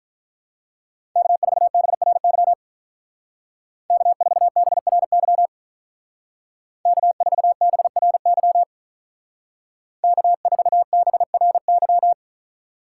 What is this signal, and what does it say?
Morse code, heard as a keyed tone, about 43 wpm, 700 Hz. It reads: K4BRY K4BRY K4BRY K4BRY
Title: Speed Racing - USA Call Signs 35wpm